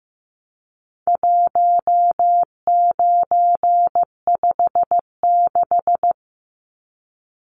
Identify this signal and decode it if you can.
Morse code, heard as a keyed tone, about 15 wpm, 700 Hz. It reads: 1956
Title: Numbers - 4-Digit 15wpm